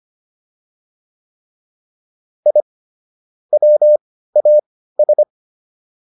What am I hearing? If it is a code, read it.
Morse code, heard as a keyed tone, about 25 wpm, spaced as for 15 wpm, 600 Hz. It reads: I WAS